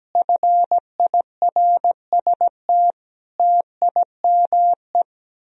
Morse code, heard as a keyed tone, about 17 wpm, 700 Hz. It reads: FIRST TIME
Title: Speed-Racing - Sets of 5 Words 17wpm